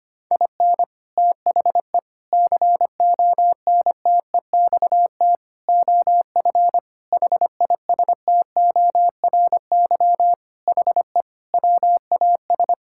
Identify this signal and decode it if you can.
Morse code, heard as a keyed tone, about 25 wpm, 700 Hz. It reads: IN THE CONTEXT OF HISTORY HE WAS
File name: practice / 25wpm / Sentences with 4-Word Phrases - 25wpm